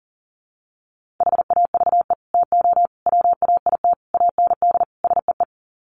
Morse code, heard as a keyed tone, about 40 wpm, 700 Hz. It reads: HAVE TO WAIT AND SEE
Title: Sets of 5 Words Encore2 40wpm